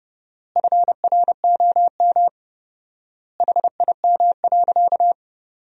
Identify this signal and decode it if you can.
Morse code, heard as a keyed tone, about 30 wpm, 700 Hz. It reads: FROM HIM.
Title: 4x Word-Spacing - Sentences from Top 400 Words - 30wpm